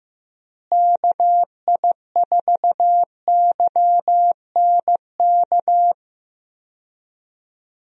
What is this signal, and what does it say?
Morse code, heard as a keyed tone, about 15 wpm, 700 Hz. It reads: KI4YNK
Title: USA Call Signs - 6 Characters - Repeated 3x - 15wpm